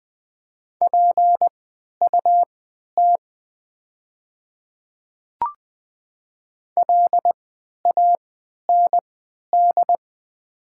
Morse code, heard as a keyed tone, about 20 wpm, spaced as for 11 wpm, 700 Hz. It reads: PUT LAND
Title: 3x Character Spacing - Top 300 Words - 20wpm